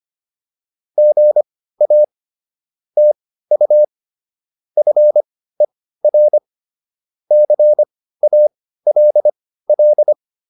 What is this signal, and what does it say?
Morse code, heard as a keyed tone, about 25 wpm, spaced as for 15 wpm, 600 Hz. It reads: GA TU FER CALL